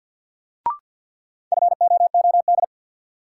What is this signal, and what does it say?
Morse code, heard as a keyed tone, about 51 wpm, 700 Hz. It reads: FOOD
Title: Sentences from Top 300 Words - New Word Drill - 50wpm